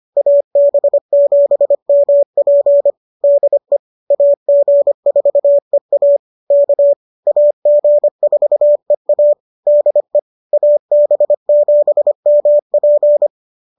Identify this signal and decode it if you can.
Morse code, heard as a keyed tone, about 25 wpm, 590 Hz. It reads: AB7MP DE AG4EA K AG4EA DE AB7MP